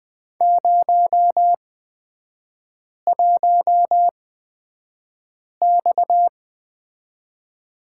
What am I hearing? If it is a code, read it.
Morse code, heard as a keyed tone, about 20 wpm, spaced as for 5 wpm, 690 Hz. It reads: 01X
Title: 229 Lesson 37e - Letter x - Three Characters 20wpm